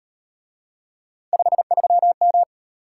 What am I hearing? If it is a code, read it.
Morse code, heard as a keyed tone, about 38 wpm, 700 Hz. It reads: 53M